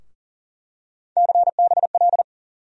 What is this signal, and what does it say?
Morse code, heard as a keyed tone, about 40 wpm, 700 Hz. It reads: CBL
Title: USA Call Sign Postfixes - 3 Letters 40wpm